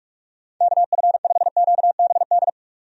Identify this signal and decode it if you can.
Morse code, heard as a keyed tone, about 42 wpm, 700 Hz. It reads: KF5QBD
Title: USA Call Signs - 6 Characters - 45wpm